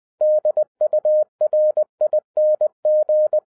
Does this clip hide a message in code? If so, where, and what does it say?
Yes: Morse code from 0.2 to 3.4 s, heard as a keyed tone, about 20 wpm, 610 Hz. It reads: DURING